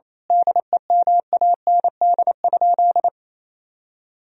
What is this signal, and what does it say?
Morse code, heard as a keyed tone, about 28 wpm, 700 Hz. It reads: DEMAND?
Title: Sentences from Top 1000 Words - Review Part 4 - 28wpm